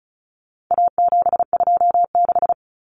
Morse code, heard as a keyed tone, about 34 wpm, 700 Hz. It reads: A726
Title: Common Cut Numbers - 4-Digit 35wpm